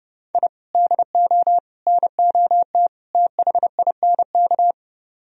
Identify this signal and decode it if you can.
Morse code, heard as a keyed tone, about 32 wpm, 700 Hz. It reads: I DO NOT THINK